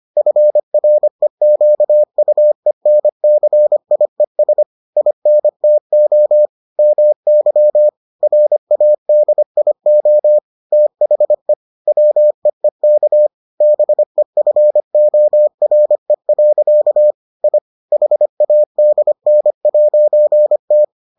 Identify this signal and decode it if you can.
Morse code, heard as a keyed tone, about 25 wpm, 600 Hz. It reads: FREQUENCIES INTO MY RADIO THE WEEK BEFORE. I HADN'T